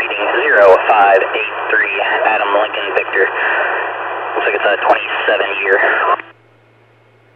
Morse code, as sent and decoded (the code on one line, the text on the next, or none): none